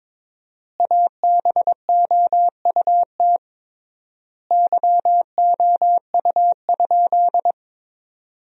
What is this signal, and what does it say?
Morse code, heard as a keyed tone, about 22 wpm, 700 Hz. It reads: ABOUT YOU?
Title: 3x Word-Spacing - Sentences from Top 100 Words - 22wpm